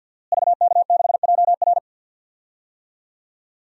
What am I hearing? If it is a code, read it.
Morse code, heard as a keyed tone, about 51 wpm, 700 Hz. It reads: VK6JR
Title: DXCC-100 Call Signs - 5 Characters - Repeated 2x - 50wpm